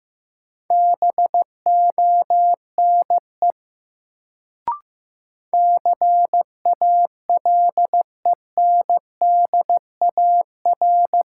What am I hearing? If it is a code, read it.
Morse code, heard as a keyed tone, about 15 wpm, 700 Hz. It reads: BONE CALENDAR
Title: Sentences from Top 1600 Words - New Word Drill Alt - 15wpm